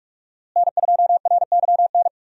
Morse code, heard as a keyed tone, about 46 wpm, 700 Hz. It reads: N2RYN
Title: USA Call Signs - 5 Characters - Repeated 3x - 45wpm